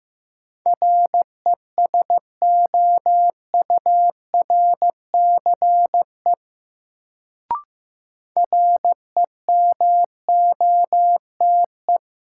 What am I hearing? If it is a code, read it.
Morse code, heard as a keyed tone, about 15 wpm, 700 Hz. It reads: RESOURCE REMOTE
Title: Sentences from Top 1300 Words - New Word Drill - 15wpm